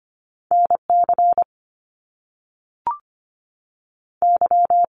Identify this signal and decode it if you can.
Morse code, heard as a keyed tone, about 25 wpm, 700 Hz. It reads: N C Y
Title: Top Two Letter Combinations in Words 25wpm